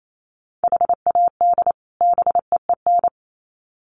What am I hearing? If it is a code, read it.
Morse code, heard as a keyed tone, about 28 wpm, 700 Hz. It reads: HAD BEEN
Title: Sets of 2 Words 28wpm